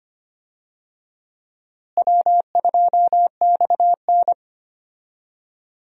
Morse code, heard as a keyed tone, about 25 wpm, 700 Hz. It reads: W2XN
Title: USA Call Signs - 4 Characters - Repeated 2x - 25wpm